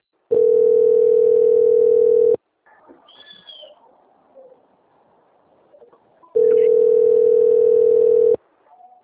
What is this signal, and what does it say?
Ringback tone (North America), 440 + 480 Hz.
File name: out- - - - - .98643